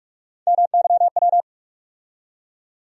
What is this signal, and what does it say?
Morse code, heard as a keyed tone, about 46 wpm, 700 Hz. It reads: MYW
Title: USA Call Sign Postfixes - 3 Letters 45wpm